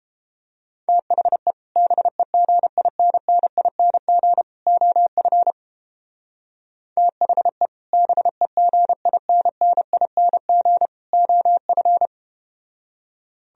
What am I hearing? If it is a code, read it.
Morse code, heard as a keyed tone, about 31 wpm, 700 Hz. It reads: THE BEGINNING OF THE BEGINNING OF